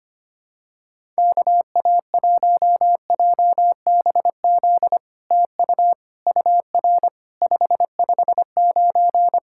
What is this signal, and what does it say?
Morse code, heard as a keyed tone, about 25 wpm, 700 Hz. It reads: KA1JBZ TU UR 559
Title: SOTA QSO Practice 25wpm